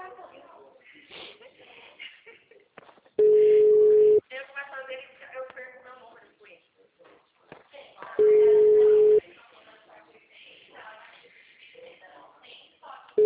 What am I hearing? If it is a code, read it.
Ringback tone (Europe, 425 Hz), with speech.
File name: out- -2001- - - .4877